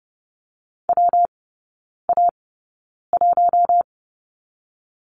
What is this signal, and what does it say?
Morse code, heard as a keyed tone, about 30 wpm, 700 Hz. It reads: W A 1
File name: Lesson 19e - Letter w - Three Characters 30wpm